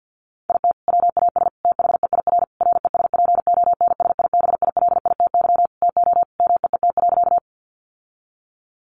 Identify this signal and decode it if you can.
Morse code, heard as a keyed tone, about 50 wpm, 700 Hz. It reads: IT WAS THEIR RESPONSIBILITY TO MEET.
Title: Sentences from Top 1200 Words 50wpm